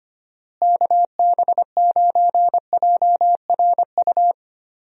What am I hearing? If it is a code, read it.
Morse code, heard as a keyed tone, about 25 wpm, 700 Hz. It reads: KB9JRU
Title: USA Call Signs - 6 Characters - Repeated 2x - 25wpm